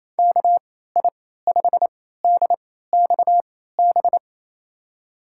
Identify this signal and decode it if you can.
Morse code, heard as a keyed tone, about 28 wpm, spaced as for 16 wpm, 700 Hz. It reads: KI5DXB
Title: USA Call Signs - 3x Character Spacing - 28wpm